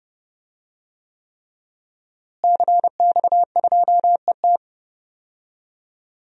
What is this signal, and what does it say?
Morse code, heard as a keyed tone, about 30 wpm, 700 Hz. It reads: CX2ET
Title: DXCC-100 Call Signs - 5 Characters - Repeated 3x - 30wpm